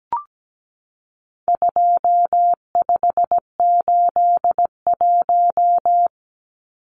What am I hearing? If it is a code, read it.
Morse code, heard as a keyed tone, about 17 wpm, 700 Hz. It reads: 2581
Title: Numbers - 4-Digit 17wpm